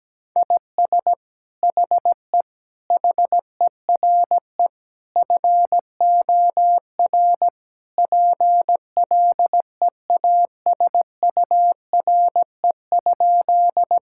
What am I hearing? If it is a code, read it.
Morse code, heard as a keyed tone, about 17 wpm, 700 Hz. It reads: IS HE HERE FOR PLEASURE?
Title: Sentences from Top 1600 Words 17wpm